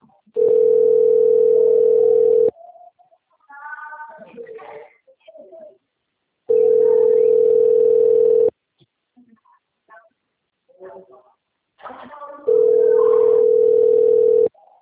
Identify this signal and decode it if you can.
Ringback tone (North America), 440 + 480 Hz, with speech.
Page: TLML PRUDENTE 1